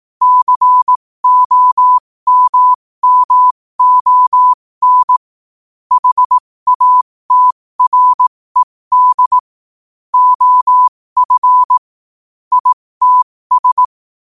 Morse code, heard as a keyed tone, about 18 wpm, spaced as for 15 wpm, 1000 Hz. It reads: COMMON HATRED OF ITS
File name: qotdfiles / 15wpm